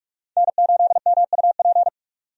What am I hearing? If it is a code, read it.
Morse code, heard as a keyed tone, about 46 wpm, 700 Hz. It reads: N8MUP
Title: USA Call Signs - 5 Characters - 45wpm